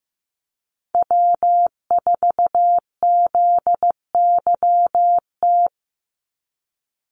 Morse code, heard as a keyed tone, about 15 wpm, 700 Hz. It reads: W4ZYT
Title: Call Signs Repeated 3x - 15wpm